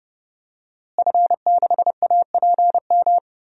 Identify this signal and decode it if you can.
Morse code, heard as a keyed tone, about 30 wpm, 700 Hz. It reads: F6APM
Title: DXCC-100 Call Signs - 5 Characters - Repeated 3x - 30wpm